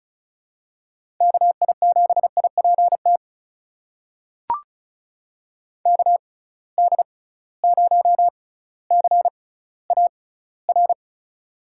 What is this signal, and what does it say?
Morse code, heard as a keyed tone, about 34 wpm, spaced as for 18 wpm, 700 Hz. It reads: KI7IPT KD0CAR
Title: USA Call Signs - 6 Characters - 6x Character Spacing - 35wpm